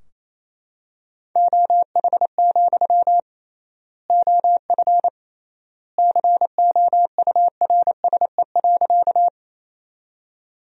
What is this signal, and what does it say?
Morse code, heard as a keyed tone, about 28 wpm, 700 Hz. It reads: OH, OF COURSE.
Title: 3x Word-Spacing - Sentences from Top 500 Words - 28wpm